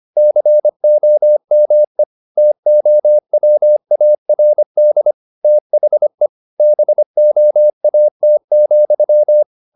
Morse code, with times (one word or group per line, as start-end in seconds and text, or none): 0.2-2.0 COME
2.4-5.1 TOWARD
5.4-6.3 THE
6.6-9.4 BOAT,